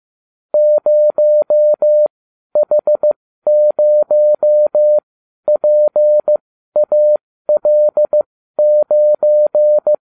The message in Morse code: 0H0PAL9

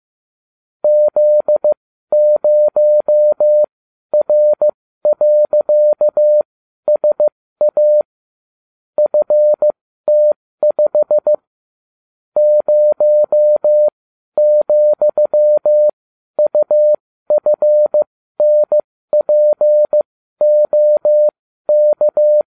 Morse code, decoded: Z0R.SA FT5 0,UFNPOK

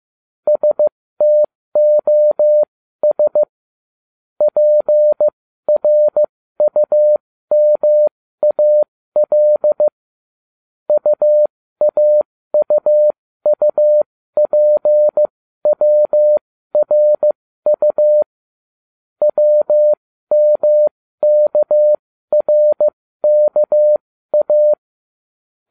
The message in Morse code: STOS PRUMAL UAUUPWRU WMKRKA

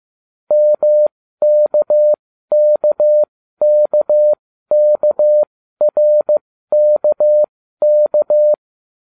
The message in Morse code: MKKKKRKK

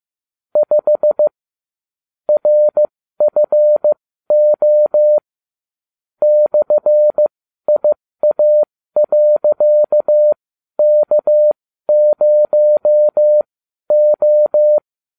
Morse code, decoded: 5 RFO /IA.K0O

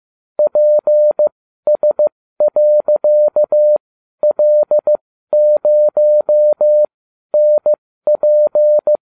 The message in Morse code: PS.L0NP

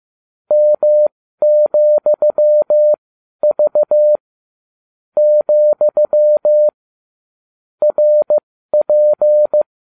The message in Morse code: M,V , RP

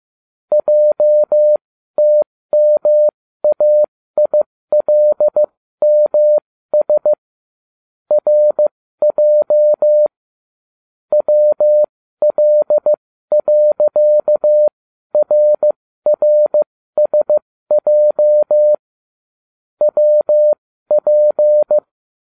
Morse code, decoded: JTMAILMS RJ WL.RRSJ WP

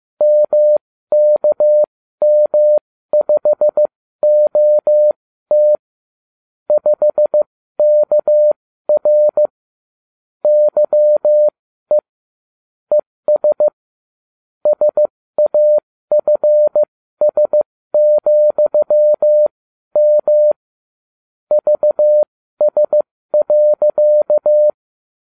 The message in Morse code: MKM5OT 5KR YE ES SAFS,M VS.

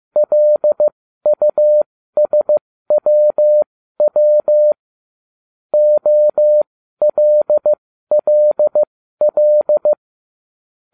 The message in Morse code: LUSWW OLLL

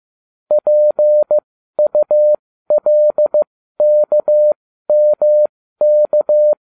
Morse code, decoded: PULKMK

